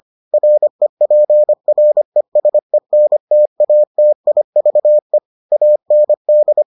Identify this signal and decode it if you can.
Morse code, heard as a keyed tone, about 25 wpm, 600 Hz. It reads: REPRESENTATIVE AND